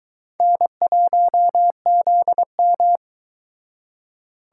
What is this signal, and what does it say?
Morse code, heard as a keyed tone, about 23 wpm, 700 Hz. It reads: N1ZM